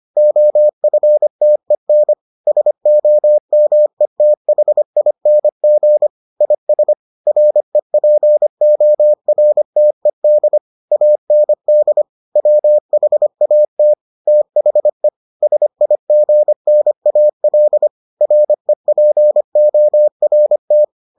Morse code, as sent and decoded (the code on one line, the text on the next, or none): --- ..-. - . -. / ... --- -- . - .... .. -. --. / .. ... / .-. .--. --- .-. - . -.. / .- -. -.. / .-- .... .- - / - .... / ... .. --. -. .- .-.. / .-. .--. --- .-. -
OFTEN SOMETHING IS REPORTED AND WHAT THE SIGNAL REPORT